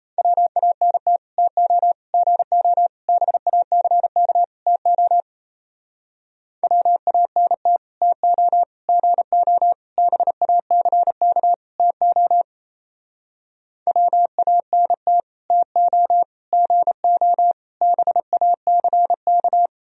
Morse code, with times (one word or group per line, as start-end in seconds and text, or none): 0.2-1.2 WANT
1.4-1.9 TO
2.1-2.9 GO
3.1-4.4 BACK
4.7-5.2 TO
6.6-7.8 WANT
8.0-8.6 TO
8.9-9.7 GO
10.0-11.5 BACK
11.8-12.4 TO
13.9-15.2 WANT
15.5-16.2 TO
16.5-17.5 GO
17.8-19.7 BACK